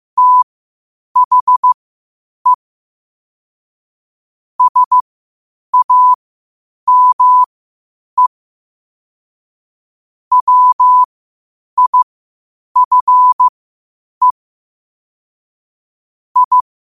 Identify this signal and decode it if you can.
Morse code, heard as a keyed tone, about 15 wpm, spaced as for 8 wpm, 1000 Hz. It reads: THE SAME WIFE I
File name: qotdfiles / 8wpm